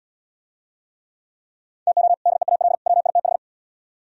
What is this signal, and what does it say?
Morse code, heard as a keyed tone, about 77 wpm, 700 Hz. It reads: TO DEAL WITH